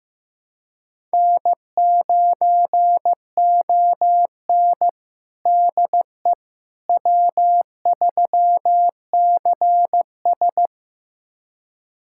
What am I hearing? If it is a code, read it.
Morse code, heard as a keyed tone, about 15 wpm, 700 Hz. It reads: N9ON DE W3CS
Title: USA Callsign DE Callsign - 4 Characters - 15wpm